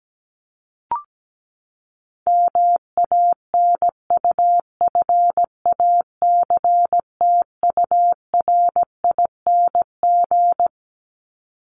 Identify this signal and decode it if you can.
Morse code, heard as a keyed tone, about 17 wpm, 700 Hz. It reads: MANUFACTURING
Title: ING Suffix - Top 100 Words with Base Words 17wpm